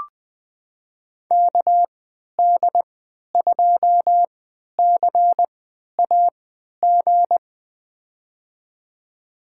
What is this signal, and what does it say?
Morse code, heard as a keyed tone, about 20 wpm, spaced as for 11 wpm, 700 Hz. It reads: KD2CAG
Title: USA Call Signs - 6 Characters - 3x Character Spacing - 20wpm